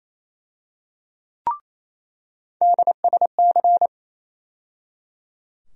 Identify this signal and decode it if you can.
Morse code, heard as a keyed tone, about 28 wpm, 700 Hz. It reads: DSC
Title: USA Call Sign Postfixes - 3 Letters 28wpm